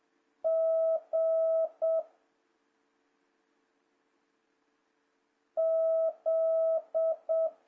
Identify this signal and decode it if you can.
Morse code, heard as a keyed tone, about 7 wpm, 650 Hz. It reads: G Z
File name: jh8cbh / 2026cw